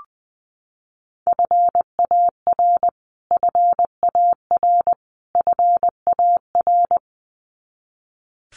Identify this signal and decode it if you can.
Morse code, heard as a keyed tone, about 19 wpm, 700 Hz. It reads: FAR FAR FAR
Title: Top 300 Words Repeated 20wpm